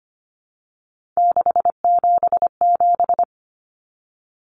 Morse code, heard as a keyed tone, about 25 wpm, 700 Hz. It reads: 677
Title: Common Cut Numbers - 3-Digit 25wpm